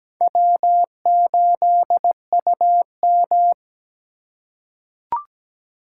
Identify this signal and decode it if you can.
Morse code, heard as a keyed tone, about 17 wpm, 700 Hz. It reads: W8UM
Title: USA Call Signs - 3x Character Spacing - 17wpm